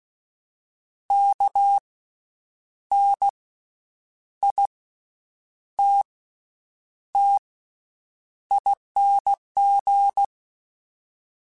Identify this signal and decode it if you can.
Morse code, heard as a keyed tone, about 16 wpm, 800 Hz. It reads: K N I T T ING
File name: Special INGs-mp3s / Special INGs-mp3s / Slow ing @15